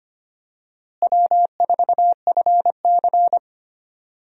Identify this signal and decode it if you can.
Morse code, heard as a keyed tone, about 25 wpm, 700 Hz. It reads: W4FC
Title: USA Call Signs - 4 Characters - Repeated 2x - 25wpm